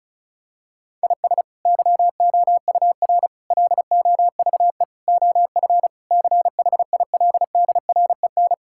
Morse code, heard as a keyed tone, about 36 wpm, 700 Hz. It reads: IS YOUR LOVE OF CHILDREN
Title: Sentences from Top 1000 Words - Review Part 5 - 35wpm